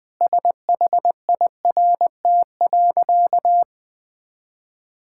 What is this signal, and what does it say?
Morse code, heard as a keyed tone, about 20 wpm, 700 Hz. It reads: SHIRT.
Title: Easy Sentences with Hints - 20wpm